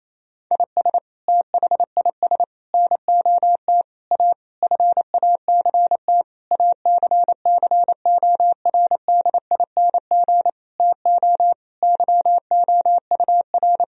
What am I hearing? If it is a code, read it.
Morse code, heard as a keyed tone, about 28 wpm, 700 Hz. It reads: IS THIS NOT A FACT ACCORDING TO YOUR
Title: Sentences from Top 1000 Words - Review Part 4 - 28wpm